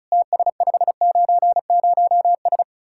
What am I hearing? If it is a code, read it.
Morse code, heard as a keyed tone, about 44 wpm, 700 Hz. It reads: TS590S